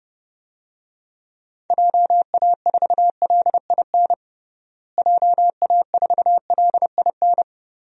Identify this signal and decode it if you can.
Morse code, heard as a keyed tone, about 32 wpm, 700 Hz. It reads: JA4LIN JA4LIN
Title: DXCC-100 Call Signs - 6 Characters - Repeated 2x - 30wpm